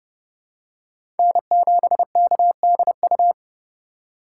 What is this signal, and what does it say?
Morse code, heard as a keyed tone, about 32 wpm, 700 Hz. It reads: N7KDU
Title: USA Call Signs - 5 Characters - 2x Character Spacing - 30wpm